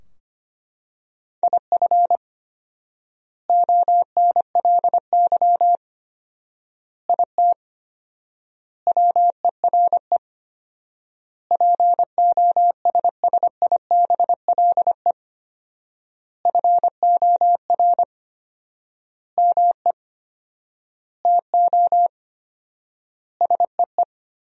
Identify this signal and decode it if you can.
Morse code, heard as a keyed tone, about 25 wpm, 700 Hz. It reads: IF ONLY IT WERE POSSIBLE FOR ME TO SEE
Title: 4x Word-Spacing - Sentences from Top 500 Words - 25wpm